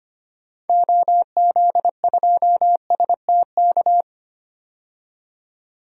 Morse code, heard as a keyed tone, about 25 wpm, 700 Hz. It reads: OZ2STK
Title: DXCC-100 Call Signs - 6 Characters - 25wpm